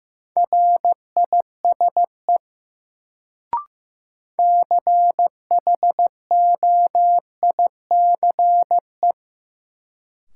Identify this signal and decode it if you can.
Morse code, heard as a keyed tone, about 15 wpm, 700 Hz. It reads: RISE CHOICE
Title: Sentences from Top 900 Words - New Word Drill - 15wpm